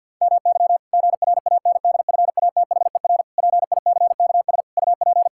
Morse code, heard as a keyed tone, about 51 wpm, 700 Hz. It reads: MY GRANDFATHER PICKS UP
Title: Pangram Challenge v2 - 50wpm